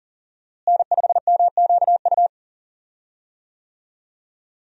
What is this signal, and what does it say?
Morse code, heard as a keyed tone, about 37 wpm, 700 Hz. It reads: N5MQU
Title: USA Call Signs - 5 Characters - Repeated 2x - 40wpm